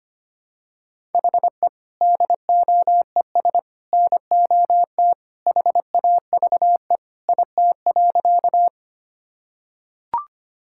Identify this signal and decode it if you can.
Morse code, heard as a keyed tone, about 25 wpm, 700 Hz. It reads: HE DOES NOT HAVE IT.
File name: practice / 25wpm / Sentences with 3-Word Phrases - 25wpm